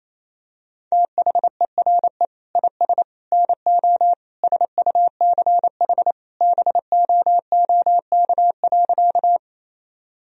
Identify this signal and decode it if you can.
Morse code, heard as a keyed tone, about 28 wpm, 700 Hz. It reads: THERE IS NO SUCH BOOK.
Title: Sentences from Top 1000 Words - Review Part 1 - 28wpm